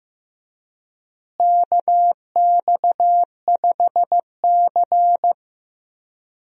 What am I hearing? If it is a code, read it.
Morse code, heard as a keyed tone, about 15 wpm, 700 Hz. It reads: KX5C
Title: USA Call Signs - 4 Characters - Repeated 2x - 15wpm